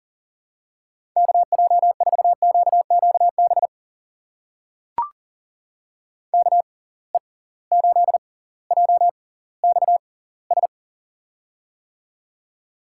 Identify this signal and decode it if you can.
Morse code, heard as a keyed tone, about 37 wpm, spaced as for 20 wpm, 700 Hz. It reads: KJ4QQB KE8JXS
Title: USA Call Signs - 6x Character Spacing - 40wpm